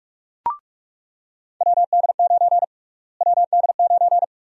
Morse code, heard as a keyed tone, about 46 wpm, 700 Hz. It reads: WD9 WD9
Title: USA Call Sign Prefixes - Repeated 2x - 45wpm